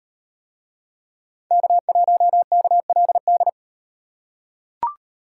Morse code, heard as a keyed tone, about 37 wpm, 700 Hz. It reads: K1KLD